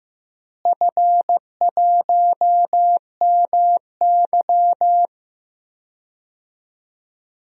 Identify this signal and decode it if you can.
Morse code, heard as a keyed tone, about 15 wpm, 700 Hz. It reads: F1MY
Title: DXCC-100 Call Signs - 4 Characters - Repeated 2x - 15wpm